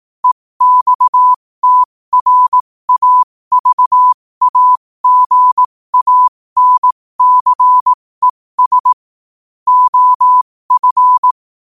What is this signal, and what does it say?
Morse code, heard as a keyed tone, about 18 wpm, spaced as for 15 wpm, 1000 Hz. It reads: EXTRAVAGANCES OF
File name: qotdfiles / 15wpm